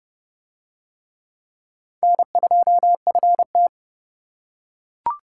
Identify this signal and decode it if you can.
Morse code, heard as a keyed tone, about 27 wpm, 700 Hz. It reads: N2FT